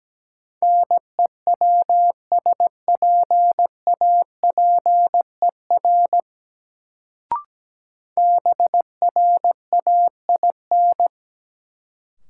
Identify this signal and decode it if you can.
Morse code, heard as a keyed tone, about 17 wpm, 700 Hz. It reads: NEWSPAPER BRAIN
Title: Sentences from Top 1100 Words - New Word Drill Alt - 17wpm